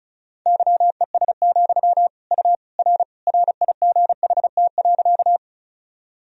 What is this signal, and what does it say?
Morse code, heard as a keyed tone, about 36 wpm, 700 Hz. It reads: YES, U R RIGHT.
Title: Sentences from Top 500 Words and Abbreviations 35wpm